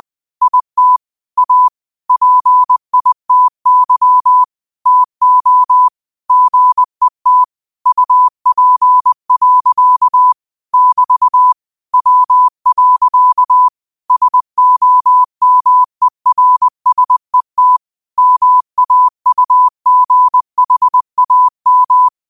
Morse code, heard as a keyed tone, about 20 wpm, 1000 Hz. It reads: IT A PITY TO GET UP. <BT> W. SOMERSET MAUGHAM
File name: qotdfiles / 20wpm